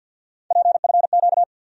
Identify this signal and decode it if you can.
Morse code, heard as a keyed tone, about 54 wpm, 700 Hz. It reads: PFQ
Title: USA Call Sign Postfixes - 3 Letters 50wpm